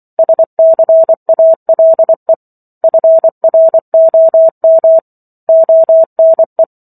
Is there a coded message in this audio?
Yes: Morse code from 0.2 to 6.6 s, heard as a keyed tone, about 24 wpm, 650 Hz. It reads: SCALE FROM ONE